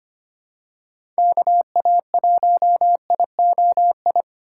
Morse code, heard as a keyed tone, about 25 wpm, 700 Hz. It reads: KA1IOI
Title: USA Call Signs - 6 Characters - 8x Character Spacing - 25wpm